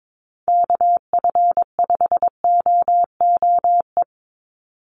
Morse code, heard as a keyed tone, about 22 wpm, 700 Hz. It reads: KF5OOE
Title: USA Call Signs 22wpm